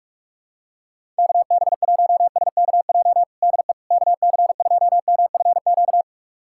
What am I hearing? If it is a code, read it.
Morse code, heard as a keyed tone, about 41 wpm, 690 Hz. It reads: KB1SKJ DE KC2MFQ
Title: USA Callsign DE Callsign 45wpm